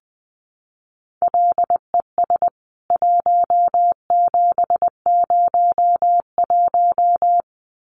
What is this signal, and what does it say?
Morse code, heard as a keyed tone, about 20 wpm, 700 Hz. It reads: LES 1701
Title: CWT Exchange 20wpm